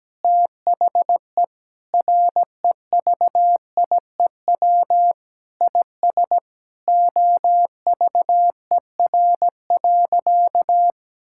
Morse code, heard as a keyed tone, about 17 wpm, 700 Hz. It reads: THE REVIEW IS OVER.
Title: Sentences from Top 1000 Words - Review Part 6 - 17wpm